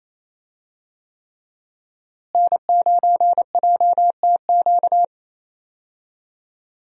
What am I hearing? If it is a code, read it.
Morse code, heard as a keyed tone, about 28 wpm, 700 Hz. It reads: N9JTQ
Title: USA Call Signs - 5 Characters - Repeated 3x - 28wpm